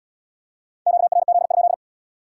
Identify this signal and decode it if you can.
Morse code, heard as a keyed tone, about 77 wpm, 700 Hz. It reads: QRZ?